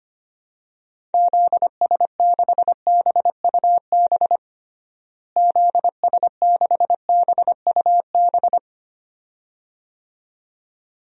Morse code, heard as a keyed tone, about 25 wpm, 700 Hz. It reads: ZS6BUB ZS6BUB
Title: DXCC-100 Call Signs - 6 Characters - Repeated 2x - 25wpm